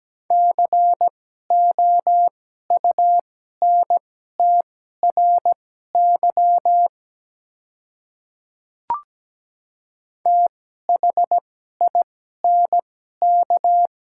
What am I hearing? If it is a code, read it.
Morse code, heard as a keyed tone, about 17 wpm, spaced as for 12 wpm, 700 Hz. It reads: COUNTRY THINK
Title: 2x Character Spacing - Top 300 Words - 17wpm